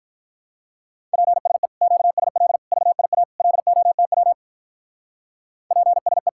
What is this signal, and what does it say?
Morse code, heard as a keyed tone, about 45 wpm, 700 Hz. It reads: PSE QSL VIA LOTW PSE